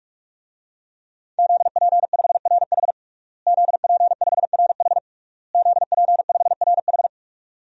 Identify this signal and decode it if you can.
Morse code, heard as a keyed tone, about 46 wpm, 700 Hz. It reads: ZP5RH ZP5RH ZP5RH
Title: DXCC-100 Call Signs - 5 Characters - Repeated 3x - 45wpm